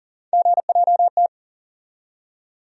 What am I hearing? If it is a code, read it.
Morse code, heard as a keyed tone, about 40 wpm, 700 Hz. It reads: GJT
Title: USA Call Sign Postfixes - 3 Letters 40wpm